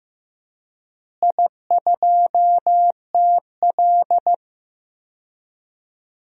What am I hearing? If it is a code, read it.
Morse code, heard as a keyed tone, about 16 wpm, 700 Hz. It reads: I2TL
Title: DXCC-100 Call Signs - 4 Characters - Repeated 3x - 15wpm